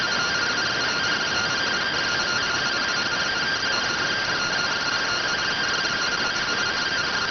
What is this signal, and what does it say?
Two-tone teleprinter signal, about 170 Hz shift.